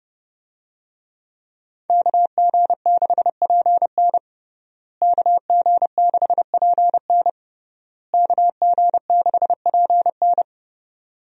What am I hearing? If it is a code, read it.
Morse code, heard as a keyed tone, about 32 wpm, 700 Hz. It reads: KG6PN KG6PN KG6PN
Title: USA Call Signs - 5 Characters - Repeated 3x - 30wpm